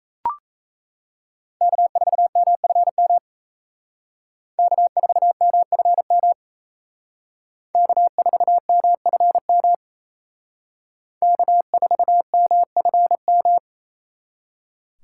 Morse code, heard as a keyed tone, about 34 wpm, 700 Hz. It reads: K4MFM K4MFM K4MFM K4MFM
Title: Speed Racing - USA 1x3 Call Signs 28wpm